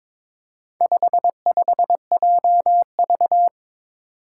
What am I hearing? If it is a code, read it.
Morse code, heard as a keyed tone, about 22 wpm, 700 Hz. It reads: 55JV
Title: Four Challenging Letter-Number Combinations 22wpm